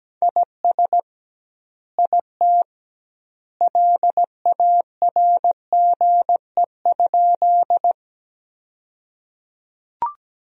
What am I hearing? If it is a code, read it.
Morse code, heard as a keyed tone, about 17 wpm, 700 Hz. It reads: IS IT LARGE?
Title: 2x Word-Spacing - Sentences from Top 200 Words - 17wpm